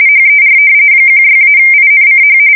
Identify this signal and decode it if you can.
RTTY, about 45.45 baud. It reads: HER 16-INCH
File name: fsk170 / battleshipnj